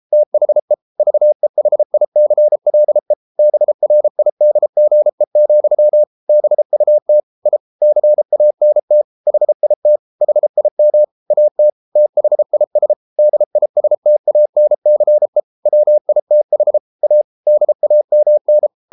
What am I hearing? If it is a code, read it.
Morse code, heard as a keyed tone, about 33 wpm, 600 Hz. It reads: THE VEHICLE BRIDGE, BUT I CANT HIT HIM AT THIS DISTANCE WITH A DAMN